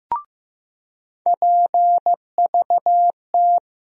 Morse code, heard as a keyed tone, about 15 wpm, 700 Hz. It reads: PVT
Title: USA Call Sign Postfixes - 3 Letters 15wpm